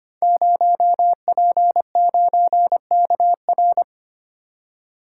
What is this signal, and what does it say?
Morse code, heard as a keyed tone, about 25 wpm, 700 Hz. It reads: 0P9KR
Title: Code Groups - 5 Letter:Numbers - 25wpm